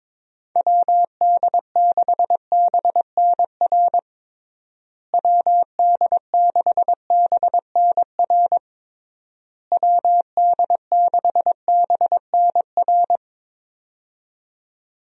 Morse code, heard as a keyed tone, about 22 wpm, 700 Hz. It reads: WD6BNR WD6BNR WD6BNR
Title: USA Call Signs - 6 Characters - Repeated 3x - 22wpm